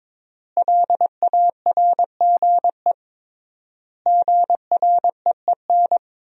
Morse code, heard as a keyed tone, about 22 wpm, 700 Hz. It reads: LARGE GREEN
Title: 3x Word-Spacing - Sentences from Top 500 Words - 22wpm